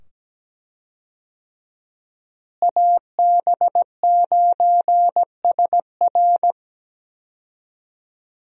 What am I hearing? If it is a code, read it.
Morse code, heard as a keyed tone, about 17 wpm, 700 Hz. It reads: AB9SR
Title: USA Call Signs - 5 Characters - Repeated 2x - 17wpm